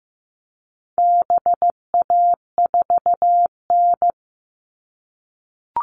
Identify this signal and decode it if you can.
Morse code, heard as a keyed tone, about 15 wpm, 700 Hz. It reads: BA4N
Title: All Cut Numbers - 4-Digit 15wpm